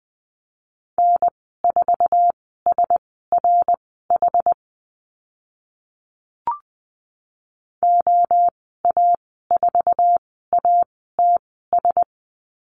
Morse code, heard as a keyed tone, about 20 wpm, spaced as for 14 wpm, 700 Hz. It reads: N4SRH OA4ATS
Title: 069 Lesson 12g - Letter d - Call Signs 20wpm